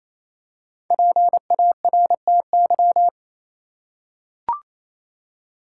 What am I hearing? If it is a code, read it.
Morse code, heard as a keyed tone, about 27 wpm, 700 Hz. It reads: PARTY
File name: Easy Words - Speed-Racing with Hints - 22wpm